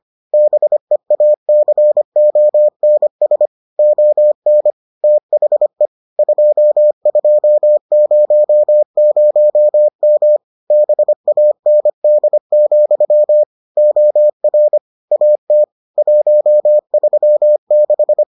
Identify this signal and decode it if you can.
Morse code, heard as a keyed tone, about 25 wpm, 600 Hz. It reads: BEACONS ON THE 2200M BAND, OR AT 136